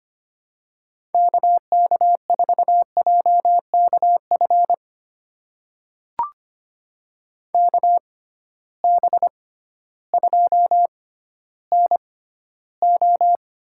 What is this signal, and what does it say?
Morse code, heard as a keyed tone, about 25 wpm, spaced as for 13 wpm, 700 Hz. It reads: KK4JKF KB2NO
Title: USA Call Signs - 6 Characters - 6x Character Spacing - 25wpm